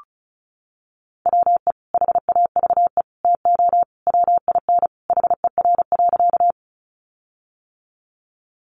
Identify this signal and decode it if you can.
Morse code, heard as a keyed tone, about 34 wpm, 700 Hz. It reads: WE HAVE TO WIN HER.
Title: Sentences from Top 1500 Words 35wpm